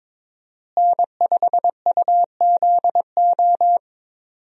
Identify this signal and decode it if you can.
Morse code, heard as a keyed tone, about 22 wpm, 700 Hz. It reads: N5UZO
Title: USA Call Signs - 5 Characters - Repeated 2x - 22wpm